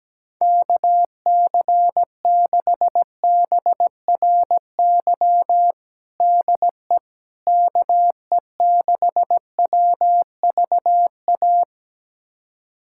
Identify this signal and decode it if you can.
Morse code, heard as a keyed tone, about 17 wpm, 700 Hz. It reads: KC6BRY DE KE6WVA
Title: USA Callsign DE Callsign - 6 Characters - 17wpm